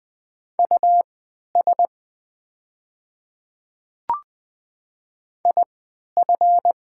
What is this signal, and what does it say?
Morse code, heard as a keyed tone, about 20 wpm, spaced as for 11 wpm, 690 Hz. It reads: US IF
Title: 3x Character Spacing - Top 200 Words - 20wpm